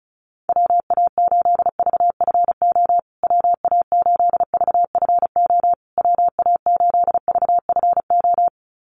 Morse code, heard as a keyed tone, about 34 wpm, 700 Hz. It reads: WA8VFO WA8VFO WA8VFO
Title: USA Call Signs Repeated 3x 35wpm